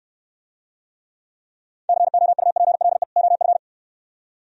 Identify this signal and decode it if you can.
Morse code, heard as a keyed tone, about 62 wpm, 700 Hz. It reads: COUPLE OF